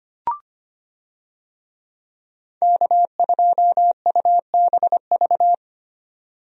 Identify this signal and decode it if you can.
Morse code, heard as a keyed tone, about 25 wpm, 700 Hz. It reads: K2UBV
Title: USA Call Signs - 5 Characters - 25wpm